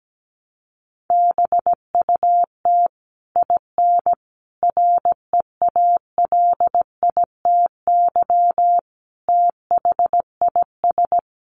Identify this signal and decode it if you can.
Morse code, heard as a keyed tone, about 17 wpm, 700 Hz. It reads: BUT IN REALITY THIS
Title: Sentences from Top 1200 Words 17wpm